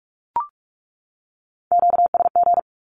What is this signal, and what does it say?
Morse code, heard as a keyed tone, about 44 wpm, 700 Hz. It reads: QSG
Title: Q-Codes 45wpm